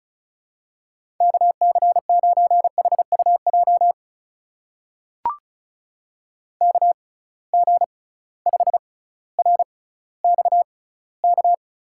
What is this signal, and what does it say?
Morse code, heard as a keyed tone, about 36 wpm, spaced as for 18 wpm, 700 Hz. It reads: KC9HUJ KG5RXK
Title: USA Call Signs - 6x Character Spacing - 35wpm